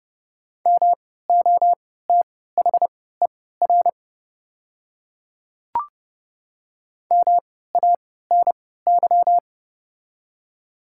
Morse code, heard as a keyed tone, about 30 wpm, spaced as for 17 wpm, 700 Hz. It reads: MOTHER MANY
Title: 3x Character Spacing - Top 200 Words - 30wpm